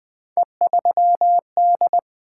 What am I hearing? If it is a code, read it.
Morse code, heard as a keyed tone, about 20 wpm, 700 Hz. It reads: E3D
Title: Three Letter-Number Combinations - 20wpm